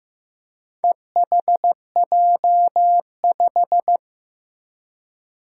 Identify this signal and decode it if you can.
Morse code, heard as a keyed tone, about 15 wpm, 700 Hz. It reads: EHJ5
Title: Four Challenging Letter-Number Combinations 15wpm